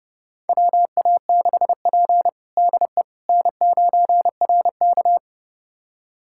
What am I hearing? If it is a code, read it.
Morse code, heard as a keyed tone, about 29 wpm, 700 Hz. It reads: WA6P DE N9RK
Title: USA Callsign DE Callsign - 4 Characters - 30wpm